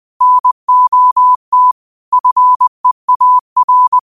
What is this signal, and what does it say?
Morse code, heard as a keyed tone, about 20 wpm, 1000 Hz. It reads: NOT FEAR
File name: qotdfiles / 20wpm